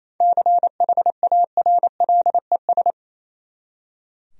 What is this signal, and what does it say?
Morse code, heard as a keyed tone, about 28 wpm, 700 Hz. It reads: CHARLES
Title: USA Common Names - 28wpm